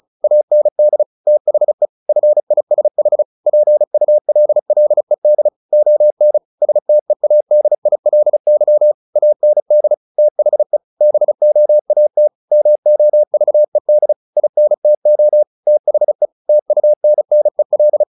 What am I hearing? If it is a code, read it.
Morse code, heard as a keyed tone, about 34 wpm, 600 Hz. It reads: AND THE FISH PULLED ON STEADILY AND THE BOAT MOVED INTO THE TUNNEL
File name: the old man and the sea 2 35wpm